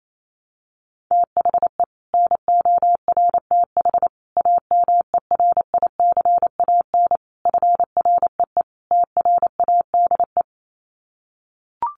Morse code, heard as a keyed tone, about 28 wpm, 700 Hz. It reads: THE NORTH AMERICAN FREE TRADE
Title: Sets of 5 Words Encore 28wpm